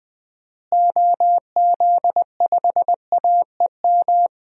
Morse code, heard as a keyed tone, about 19 wpm, 700 Hz. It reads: OZ5AEM
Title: DXCC-100 Call Signs - 6 Characters - Repeated 3x - 20wpm